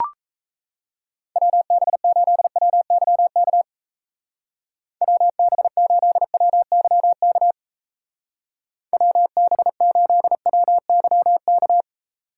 Morse code, heard as a keyed tone, about 34 wpm, 700 Hz. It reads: WB8WYK WB8WYK WB8WYK
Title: Speed Racing - USA 2x3 Call Signs 28wpm